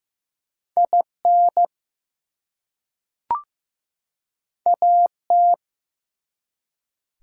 Morse code, heard as a keyed tone, about 15 wpm, 700 Hz. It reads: IN AT
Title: Two Letter Words - 15wpm